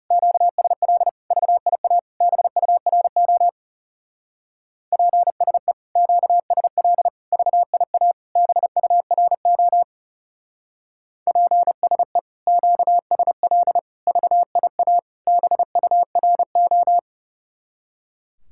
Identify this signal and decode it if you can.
Morse code, heard as a keyed tone, about 39 wpm, 700 Hz. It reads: QSL VIA BURO PSE QSL VIA BURO PSE QSL VIA BURO